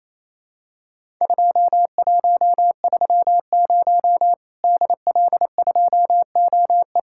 Morse code, heard as a keyed tone, about 28 wpm, 700 Hz. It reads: 2130 DL2OE
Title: Worked All Europe DX Contest - QTC Exchange - 28wpm